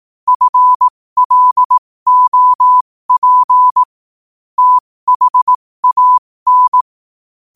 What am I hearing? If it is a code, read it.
Morse code, heard as a keyed tone, about 18 wpm, spaced as for 15 wpm, 1000 Hz. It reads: FLOP THAN
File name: qotdfiles / 15wpm